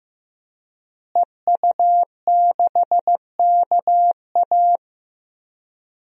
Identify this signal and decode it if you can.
Morse code, heard as a keyed tone, about 15 wpm, 700 Hz. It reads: EU6KA
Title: DXCC-100 Call Signs - 5 Characters - Repeated 2x - 15wpm